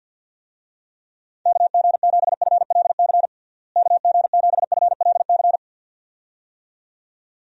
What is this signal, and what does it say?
Morse code, heard as a keyed tone, about 50 wpm, 700 Hz. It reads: KG7FLC KG7FLC
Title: USA Call Signs - 6 Characters - Repeated 2x - 50wpm